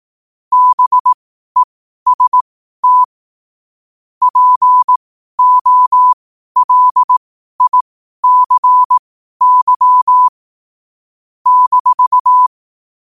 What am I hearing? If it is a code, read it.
Morse code, heard as a keyed tone, about 18 wpm, spaced as for 12 wpm, 1000 Hz. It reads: BEST POLICY -
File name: qotdfiles / 12wpm